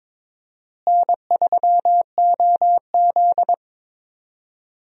Morse code, heard as a keyed tone, about 22 wpm, 700 Hz. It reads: N3OZ
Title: USA Call Signs - 4 Characters - Repeated 2x - 22wpm